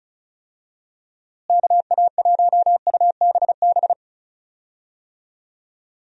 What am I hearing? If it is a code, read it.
Morse code, heard as a keyed tone, about 34 wpm, 700 Hz. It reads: KA1UBB
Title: USA Call Signs - 6 Characters - Repeated 2x - 35wpm